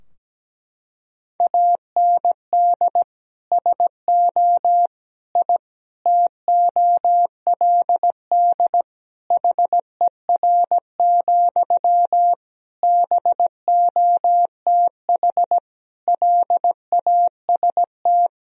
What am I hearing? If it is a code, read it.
Morse code, heard as a keyed tone, about 17 wpm, 700 Hz. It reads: AND SO I TOLD HER, BOTH LAST
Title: Sentences from Top 1000 Words - Review Part 4 - 17wpm